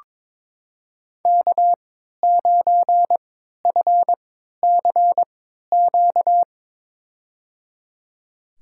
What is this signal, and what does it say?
Morse code, heard as a keyed tone, about 22 wpm, spaced as for 12 wpm, 700 Hz. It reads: K9FCQ
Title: USA Call Signs - 3x Character Spacing - 22wpm